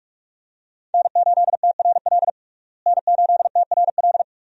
Morse code, heard as a keyed tone, about 44 wpm, 700 Hz. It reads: N8TRL N8TRL
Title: USA Call Signs - 5 Characters - Repeated 2x - 45wpm